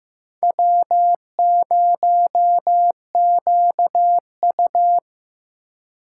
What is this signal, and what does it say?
Morse code, heard as a keyed tone, about 18 wpm, 700 Hz. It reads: W0QU
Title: Speed Racing - USA 1x2 Call Signs 15wpm